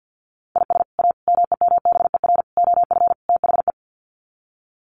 Morse code, heard as a keyed tone, about 50 wpm, 700 Hz. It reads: IS A MEMBER OF THE
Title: Sets of 5 Words 50wpm